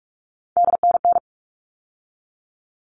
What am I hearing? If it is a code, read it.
Morse code, heard as a keyed tone, about 44 wpm, 700 Hz. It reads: DNN
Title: All Cut Numbers - 3-Digit 45wpm